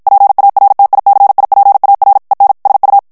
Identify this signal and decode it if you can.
Morse code, heard as a keyed tone, about 50 wpm, 800 Hz. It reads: PARTICIPAR EN SU
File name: ESPAGNOL TEXTE EN CLAIR 1 VITESSE 50 MOTS MINUTES